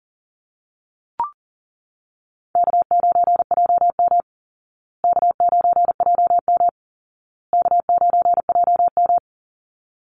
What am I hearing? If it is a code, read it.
Morse code, heard as a keyed tone, about 40 wpm, 700 Hz. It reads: K9JM K9JM K9JM
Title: Call Signs Repeated 2x - 40wpm